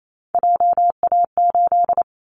Morse code, heard as a keyed tone, about 28 wpm, 700 Hz. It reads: JA8
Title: Callsign Prefixes for Top 25 DXCC - 28wpm